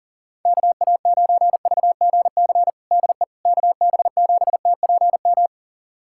Morse code, heard as a keyed tone, about 40 wpm, 700 Hz. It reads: KA9VGC DE KB7TPM
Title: USA Callsign DE Callsign - 6 Characters - 40wpm